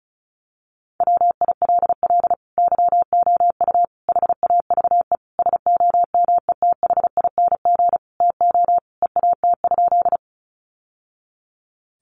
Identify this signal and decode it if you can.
Morse code, heard as a keyed tone, about 34 wpm, 700 Hz. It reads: WILL YOU HAVE SOMETHING TO EAT?
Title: Sentences from Top 600 Words 35wpm